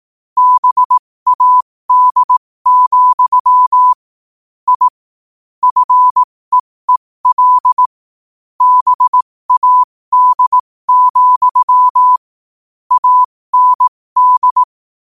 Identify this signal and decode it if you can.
Morse code, heard as a keyed tone, about 18 wpm, spaced as for 15 wpm, 1000 Hz. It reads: BAD, I FEEL BAD, AND
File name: qotdfiles / 15wpm